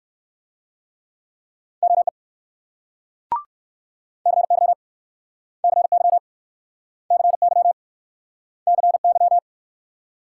Morse code, heard as a keyed tone, about 61 wpm, 700 Hz. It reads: OE CY CY CY CY